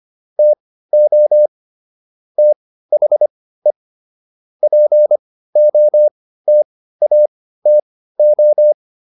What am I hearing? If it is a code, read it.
Morse code, heard as a keyed tone, about 25 wpm, spaced as for 15 wpm, 600 Hz. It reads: TO THE POTATO